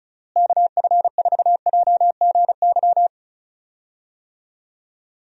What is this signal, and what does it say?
Morse code, heard as a keyed tone, about 36 wpm, 700 Hz. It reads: KF4JGY
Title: USA Call Signs - 6 Characters - Repeated 2x - 35wpm